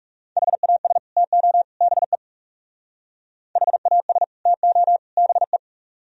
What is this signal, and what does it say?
Morse code, heard as a keyed tone, about 39 wpm, 700 Hz. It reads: HAS TO BE HAS TO BE